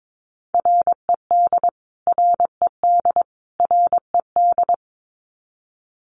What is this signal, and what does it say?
Morse code, heard as a keyed tone, about 22 wpm, 700 Hz. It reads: RED RED RED
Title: Top 500 Words Repeated 22wpm